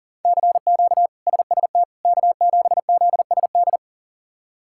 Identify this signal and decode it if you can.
Morse code, heard as a keyed tone, about 40 wpm, 700 Hz. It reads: CQ SST K7ZSD
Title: K1USN SST QSO Practice - 40wpm